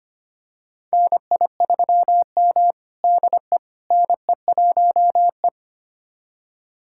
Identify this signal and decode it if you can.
Morse code, heard as a keyed tone, about 25 wpm, 700 Hz. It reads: NI3M DE NE1E
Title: USA Callsign DE Callsign - 4 Characters - 25wpm